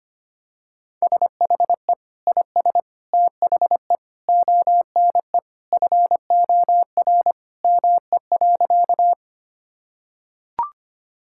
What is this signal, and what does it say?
Morse code, heard as a keyed tone, about 25 wpm, 700 Hz. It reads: SHE IS THE ONE FOR ME.
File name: practice / 25wpm / Sentences with 2-Word Phrases - 25wpm